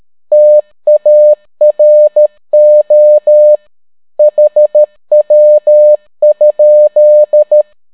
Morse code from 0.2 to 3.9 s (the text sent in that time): TARO